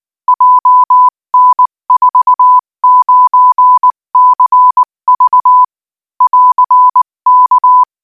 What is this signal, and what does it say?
Morse code, heard as a keyed tone, about 19 wpm, 1000 Hz. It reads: JN49CV <AR>K
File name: IZ4BEH-13-cm-AS